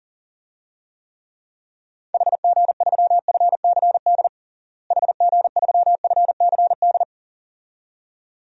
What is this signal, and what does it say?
Morse code, heard as a keyed tone, about 37 wpm, 700 Hz. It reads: HG3FCD HG3FCD
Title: DXCC-100 Call Signs - 6 Characters - Repeated 2x - 40wpm